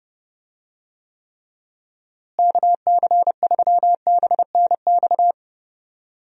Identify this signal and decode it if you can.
Morse code, heard as a keyed tone, about 30 wpm, 700 Hz. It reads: KC3BNX
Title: USA Call Signs - 6 Characters - Repeated 2x - 30wpm